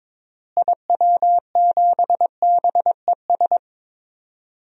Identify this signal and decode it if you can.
Morse code, heard as a keyed tone, about 22 wpm, 700 Hz. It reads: IW7BES
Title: DXCC-100 Call Signs - 6 Characters - Repeated 3x - 22wpm